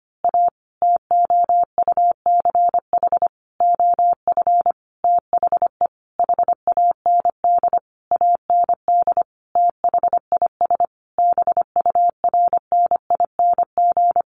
Morse code, heard as a keyed tone, about 25 wpm, 700 Hz. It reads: A TOUCH OF THE HAND AND THIS BURNING
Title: A Sound Of Thunder 25wpm